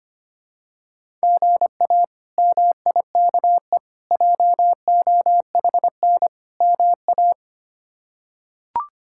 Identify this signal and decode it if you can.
Morse code, heard as a keyed tone, about 25 wpm, 700 Hz. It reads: GA MIKE JOHN MA
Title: K1USN SST QSO Practice - 25wpm